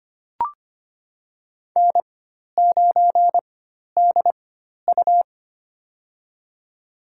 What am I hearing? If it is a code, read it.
Morse code, heard as a keyed tone, about 25 wpm, spaced as for 12 wpm, 700 Hz. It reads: N9DU
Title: USA Call Signs - 4 Characters - 4x Character Spacing - 25wpm